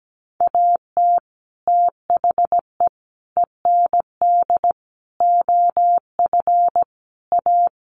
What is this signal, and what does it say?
Morse code, heard as a keyed tone, about 17 wpm, 700 Hz. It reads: AT THE END OF A
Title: Sets of 5 Words 17wpm